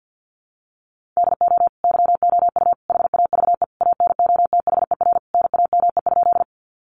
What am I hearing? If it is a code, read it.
Morse code, heard as a keyed tone, about 50 wpm, 700 Hz. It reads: DO YOU HAVE ANOTHER NAME?
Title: Sentences from Top 600 Words 50wpm